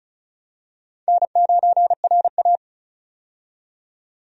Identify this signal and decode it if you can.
Morse code, heard as a keyed tone, about 36 wpm, 700 Hz. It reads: N9RA
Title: USA Call Signs - 4 Characters - 35wpm